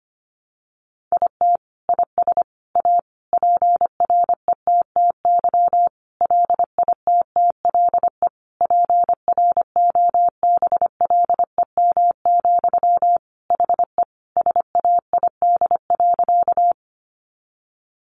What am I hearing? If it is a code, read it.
Morse code, heard as a keyed tone, about 25 wpm, 700 Hz. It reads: IT IS A PRETTY LITTLE PROBLEM, HE SAID.
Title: Sentences from Top 1000 Words 25wpm